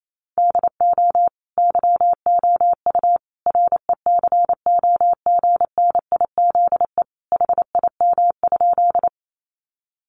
Morse code, heard as a keyed tone, about 28 wpm, 700 Hz. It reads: DO YOU RECOGNIZE HIM?
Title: Sentences from Top 1400 Words 28wpm